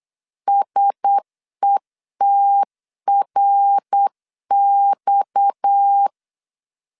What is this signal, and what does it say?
Morse code, heard as a keyed tone, about 8 wpm, 800 Hz. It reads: SETRX
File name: LEZIONI / 6 PKX0 / 6 PKX 8wpm 3041